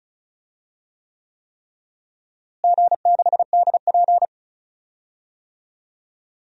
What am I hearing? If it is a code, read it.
Morse code, heard as a keyed tone, about 36 wpm, 700 Hz. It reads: G6DP
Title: DXCC-100 Call Signs - 4 Characters - Repeated 3x - 35wpm